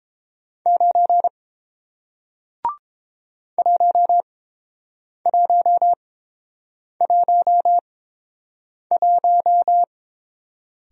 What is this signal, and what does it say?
Morse code, heard as a keyed tone, about 30 wpm, spaced as for 7 wpm, 700 Hz. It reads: 9 1111